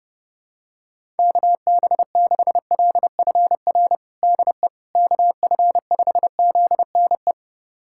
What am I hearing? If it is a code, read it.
Morse code, heard as a keyed tone, about 30 wpm, 700 Hz. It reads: KB6LFR DE KF5ZNE
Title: USA Callsign DE Callsign 30wpm